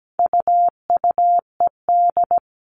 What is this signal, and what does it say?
Morse code, heard as a keyed tone, about 17 wpm, 700 Hz. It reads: UUED
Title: All Cut Numbers - 4-Digit 17wpm